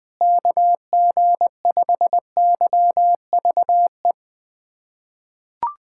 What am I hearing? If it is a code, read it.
Morse code, heard as a keyed tone, about 20 wpm, 700 Hz. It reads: KG5YVE
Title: USA Callsign DE Callsign - 6 Characters - 20wpm